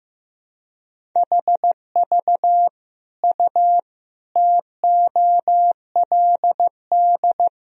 Morse code, heard as a keyed tone, about 15 wpm, 700 Hz. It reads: HV U TOLD
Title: Sentences from Top 400 Words and Abbreviations 15wpm